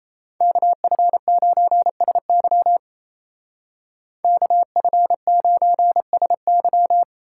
Morse code, heard as a keyed tone, about 34 wpm, 700 Hz. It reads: KF9SY KF9SY
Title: Speed Racing - USA 2x2 Call Signs 28wpm